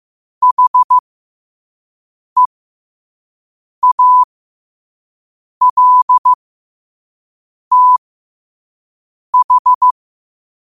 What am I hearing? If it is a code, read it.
Morse code, heard as a keyed tone, about 15 wpm, spaced as for 5 wpm, 1000 Hz. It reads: HEALTH
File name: qotdfiles / 5wpm